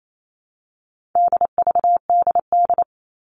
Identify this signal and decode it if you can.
Morse code, heard as a keyed tone, about 28 wpm, 700 Hz. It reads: DVDD
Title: All Cut Numbers - 4-Digit 28wpm